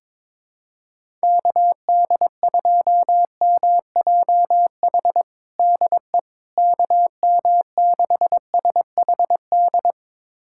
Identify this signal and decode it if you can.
Morse code, heard as a keyed tone, about 22 wpm, 700 Hz. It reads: KD2MJH DE KM6SHD
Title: USA Callsign DE Callsign 22wpm